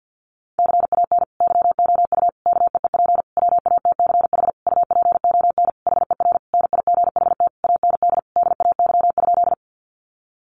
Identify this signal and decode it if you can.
Morse code, heard as a keyed tone, about 50 wpm, 700 Hz. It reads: CAN YOU KEEP WATCH UPON HER NIGHT AND DAY?
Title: Sentences from Top 1100 Words 50wpm